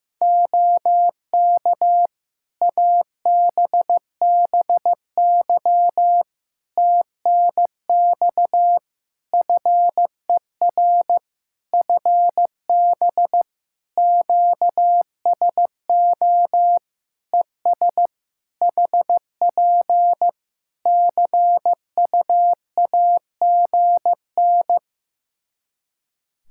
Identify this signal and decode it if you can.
Morse code, heard as a keyed tone, about 15 wpm, 700 Hz. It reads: OK ABBY TNX FER FB QSO ES HP CUAGN
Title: Practice QSO - Exchange 4 - Ending - 15wpm